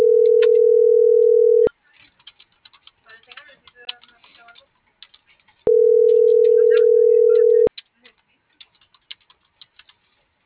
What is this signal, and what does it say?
Ringback tone (North America), 440 + 480 Hz.